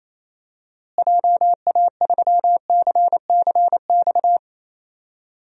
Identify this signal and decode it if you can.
Morse code, heard as a keyed tone, about 28 wpm, 700 Hz. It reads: JA3CCX
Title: DXCC-100 Call Signs - 6 Characters - Repeated 3x - 28wpm